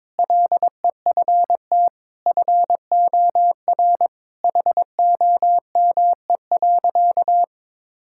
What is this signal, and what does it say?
Morse code, heard as a keyed tone, about 22 wpm, 700 Hz. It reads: LEFT FOR HOME.
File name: practice / 22wpm / Sentences with 4-Word Phrases - 22wpm